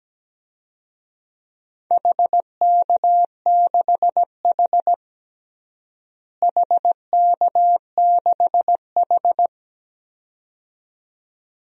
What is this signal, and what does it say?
Morse code, heard as a keyed tone, about 17 wpm, 700 Hz. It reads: HK6H HK6H
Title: DXCC-100 Call Signs - 4 Characters - Repeated 2x - 17wpm